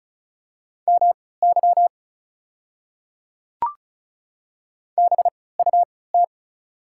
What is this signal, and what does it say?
Morse code, heard as a keyed tone, about 36 wpm, spaced as for 20 wpm, 690 Hz. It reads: MY BUT
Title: 3x Character Spacing - Top 100 Words - 35wpm